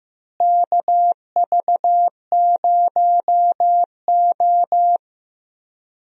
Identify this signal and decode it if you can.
Morse code, heard as a keyed tone, about 15 wpm, 700 Hz. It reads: KV0O
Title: USA Call Signs - 4 Characters - Repeated 2x - 15wpm